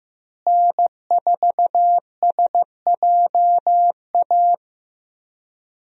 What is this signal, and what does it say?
Morse code, heard as a keyed tone, about 15 wpm, 700 Hz. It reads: N4SJA
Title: USA Call Signs - 5 Characters - Repeated 2x - 15wpm